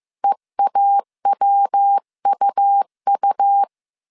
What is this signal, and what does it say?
Morse code, heard as a keyed tone, about 15 wpm, 800 Hz. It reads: EAWUU
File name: CW / DET / 1 3 1500C S A 14wpm 2447